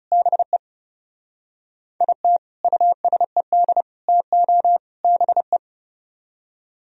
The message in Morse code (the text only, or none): BE IT USED TO BE